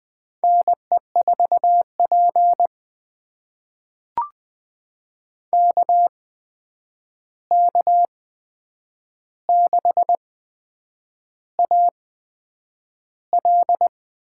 Morse code, heard as a keyed tone, about 20 wpm, spaced as for 8 wpm, 700 Hz. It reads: NE4P KK6AL
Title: USA Call Signs - 8x Character Spacing - 20wpm